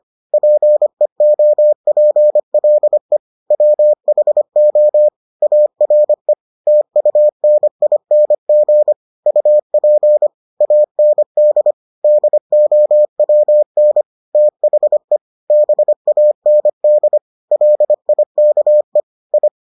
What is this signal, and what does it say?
Morse code, heard as a keyed tone, about 25 wpm, 600 Hz. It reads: PEOPLE WHO ARE TUNING UP AND DOWN THE BAND LIKE I